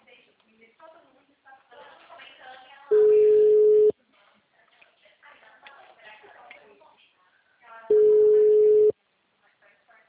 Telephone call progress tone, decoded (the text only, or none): ringback tone